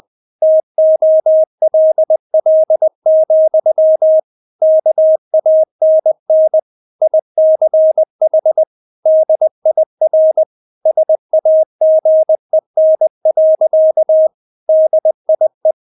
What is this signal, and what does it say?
Morse code, heard as a keyed tone, about 20 wpm, 630 Hz. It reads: TOLL, KANN ICH DIR SAGEN. DIE
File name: Witz1 20wpm